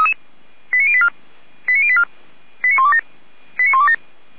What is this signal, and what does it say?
Five-tone selective call (ZVEI), heard as quick runs of five tones, tones of about 70 ms. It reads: 89084 89084 89127 89127